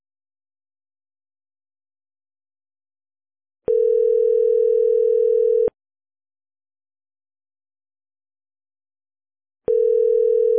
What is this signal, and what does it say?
Ringback tone (North America), 440 + 480 Hz.